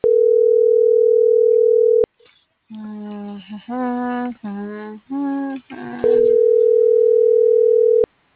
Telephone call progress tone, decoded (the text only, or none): ringback tone